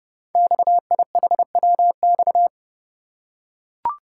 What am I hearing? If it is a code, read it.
Morse code, heard as a keyed tone, about 28 wpm, 700 Hz. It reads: XIHWX
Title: Code Groups - 5 Letters - 30wpm